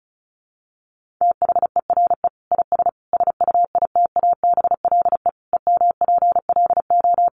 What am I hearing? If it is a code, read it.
Morse code, heard as a keyed tone, about 34 wpm, 700 Hz. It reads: THERE IS SUITABLE EMPLO
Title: Sentences from Top 1500 Words 35wpm